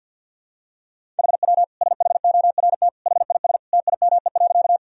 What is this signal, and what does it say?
Morse code, heard as a keyed tone, about 52 wpm, 700 Hz. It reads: HW SHORT HIS TIME.
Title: Sentences from Top 400 Words and Abbreviations 50wpm